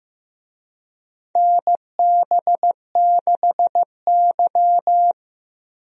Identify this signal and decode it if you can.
Morse code, heard as a keyed tone, about 15 wpm, 700 Hz. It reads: NB6Y
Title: USA Call Signs - 4 Characters - 15wpm